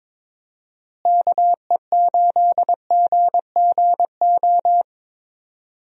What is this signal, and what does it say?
Morse code, heard as a keyed tone, about 22 wpm, 700 Hz. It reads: KE8GGO
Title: USA Call Signs - 6 Characters - Repeated 2x - 22wpm